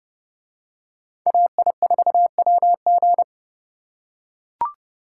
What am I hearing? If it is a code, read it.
Morse code, heard as a keyed tone, about 30 wpm, 700 Hz. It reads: AI4WG